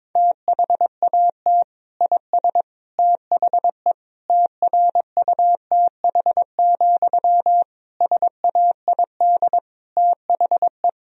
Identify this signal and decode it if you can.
Morse code, heard as a keyed tone, about 22 wpm, 700 Hz. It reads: THAT IS THE TRUTH, SAID THE
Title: Sentences from Top 1000 Words - Review Part 1 - 22wpm